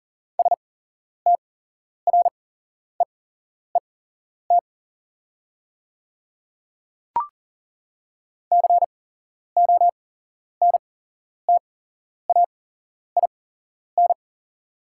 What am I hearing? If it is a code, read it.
Morse code, heard as a keyed tone, about 39 wpm, spaced as for 11 wpm, 700 Hz. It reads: STREET CONTAIN
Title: 8x Character Spacing - Top 500 Words - 40wpm